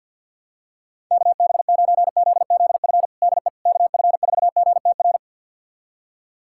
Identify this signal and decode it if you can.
Morse code, heard as a keyed tone, about 50 wpm, 700 Hz. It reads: KB9ZZF DE KF4GTR
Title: USA Callsign DE Callsign - 6 Characters - 50wpm